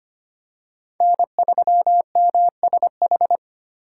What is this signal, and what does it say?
Morse code, heard as a keyed tone, about 25 wpm, 700 Hz. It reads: N3MSH
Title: USA Call Signs - 5 Characters - 2x Character Spacing - 25wpm